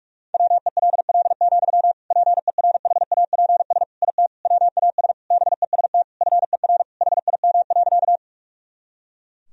Morse code, heard as a keyed tone, about 41 wpm, 700 Hz. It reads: WELL, PERHAPS IT WAS BEST FER HIM.
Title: Sentences from Top 500 Words and Abbreviations 45wpm